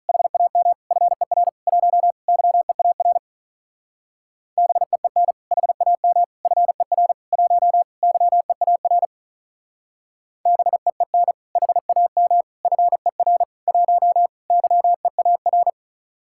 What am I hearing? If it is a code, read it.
Morse code, heard as a keyed tone, about 44 wpm, 700 Hz. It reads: HAM FER 1 YEAR BEEN HAM FER 1 YEAR BEEN HAM FER 1 YEAR